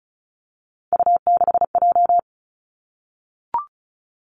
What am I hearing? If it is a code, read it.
Morse code, heard as a keyed tone, about 36 wpm, 700 Hz. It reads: U6J
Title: Three Challenging Letter-Number Combinations 35wpm